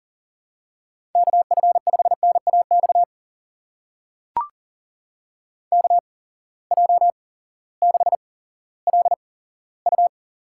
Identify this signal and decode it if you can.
Morse code, heard as a keyed tone, about 38 wpm, spaced as for 18 wpm, 700 Hz. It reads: KF5NAX KJ6LU
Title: USA Call Signs - 6 Characters - 8x Character Spacing - 40wpm